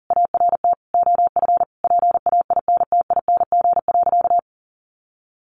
Morse code, heard as a keyed tone, about 40 wpm, 700 Hz. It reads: ART OF PAINTING.
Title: Sentences from Top 1200 Words 40wpm